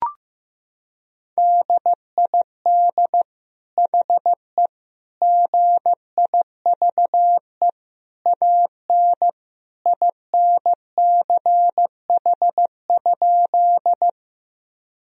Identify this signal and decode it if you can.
Morse code, heard as a keyed tone, about 15 wpm, 700 Hz. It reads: DID HE GIVE AN INCH?
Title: Sentences from Top 1000 Words - Review Part 3 - 15wpm